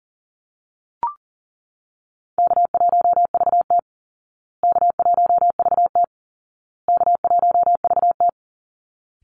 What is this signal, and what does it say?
Morse code, heard as a keyed tone, about 37 wpm, 700 Hz. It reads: K1VT K1VT K1VT
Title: Easy Call Signs Repeated 3x - 40wpm